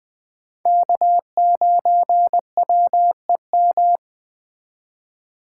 Morse code, heard as a keyed tone, about 20 wpm, 700 Hz. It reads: K9WEM
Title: Speed Racing - USA 1x3 Call Signs 17wpm